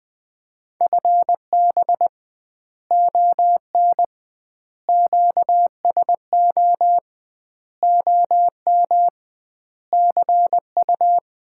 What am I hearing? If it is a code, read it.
Morse code, heard as a keyed tone, about 20 wpm, 700 Hz. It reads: FB ON QSO OM CU
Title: Common QSO Phrases - 2x Word-Spacing - 20wpm